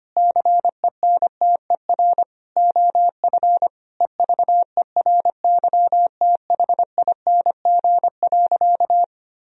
Morse code, heard as a keyed tone, about 25 wpm, 700 Hz. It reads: CENTER OF EVERYTHING.
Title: Sentences from Top 1000 Words - Review Part 2 - 25wpm